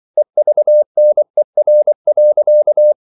Morse code, heard as a keyed tone, about 24 wpm, 600 Hz. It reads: EVNER.
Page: Ferie 24 wpm